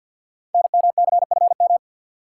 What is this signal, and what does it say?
Morse code, heard as a keyed tone, about 53 wpm, 700 Hz. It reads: NMCFM
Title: Code Groups - 5 Letters - 50wpm